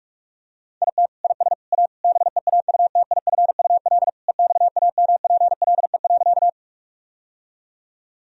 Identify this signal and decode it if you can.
Morse code, heard as a keyed tone, about 46 wpm, 700 Hz. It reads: IT IS A BEAUTIFUL EXAMPLE.
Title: Sentences from Top 1000 Words - Review Part 1 - 45wpm